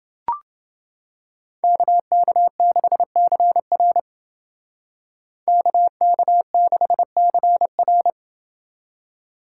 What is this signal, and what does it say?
Morse code, heard as a keyed tone, about 29 wpm, 700 Hz. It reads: KK6CR KK6CR